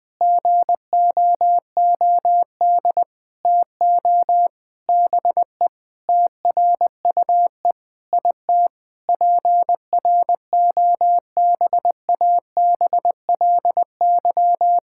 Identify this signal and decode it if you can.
Morse code, heard as a keyed tone, about 21 wpm, 700 Hz. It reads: GOOD TO BE TRUE IT PROBABLY
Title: Sentences with 9-Word Phrases - Spoken Beginning - 20wpm